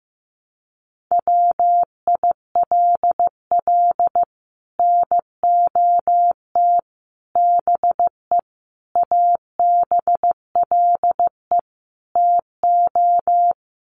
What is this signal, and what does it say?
Morse code, heard as a keyed tone, about 15 wpm, 700 Hz. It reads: WILL NOT BE ABLE TO
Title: Sets of 5 Words 15wpm